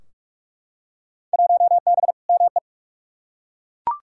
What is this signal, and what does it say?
Morse code, heard as a keyed tone, about 42 wpm, 700 Hz. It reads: 1B ME
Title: Field Day Exchanges - 45wpm